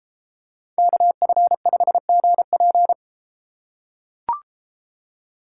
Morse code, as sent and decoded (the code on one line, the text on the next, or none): -.- ..-. ..... --. .--.
KF5GP